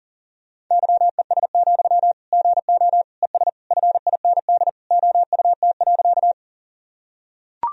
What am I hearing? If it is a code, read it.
Morse code, heard as a keyed tone, about 37 wpm, 700 Hz. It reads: YES, GO ES FIND OUT.
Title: Sentences from Top 500 Words and Abbreviations 40wpm